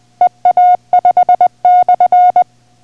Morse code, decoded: EA5/